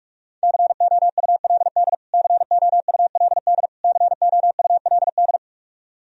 Morse code, heard as a keyed tone, about 47 wpm, 700 Hz. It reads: COULD COULD COULD
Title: Top 100 Words Repeated 45wpm